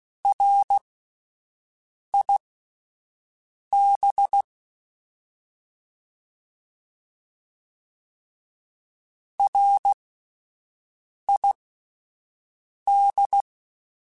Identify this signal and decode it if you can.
Morse code, heard as a keyed tone, about 16 wpm, spaced as for 5 wpm, 800 Hz. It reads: RIB RID